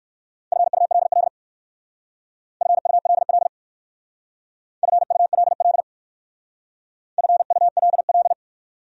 Morse code, heard as a keyed tone, about 56 wpm, 700 Hz. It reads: FULL FULL FULL FULL